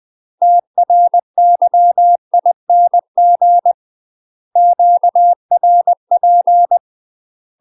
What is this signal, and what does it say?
Morse code, heard as a keyed tone, about 20 wpm, 700 Hz. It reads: TRYING QRP
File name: UK MASTER QSOs TRAINING / 20WPM / UK MASTER QSOs 0013